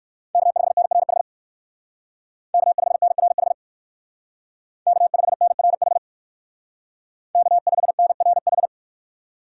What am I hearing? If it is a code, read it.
Morse code, heard as a keyed tone, about 58 wpm, 700 Hz. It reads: K5NRH K5NRH K5NRH K5NRH